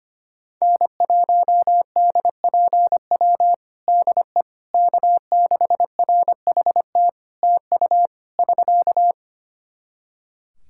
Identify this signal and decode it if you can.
Morse code, heard as a keyed tone, about 25 wpm, 700 Hz. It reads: N1DPW DE K6RHT TU <SK>
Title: Practice QSO - Exchange 4 - Ending - 25wpm